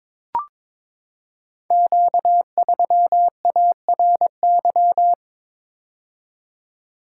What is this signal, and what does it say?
Morse code, heard as a keyed tone, about 23 wpm, 700 Hz. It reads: Q3ARY
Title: Code Groups - 5 Letter:Numbers - 22wpm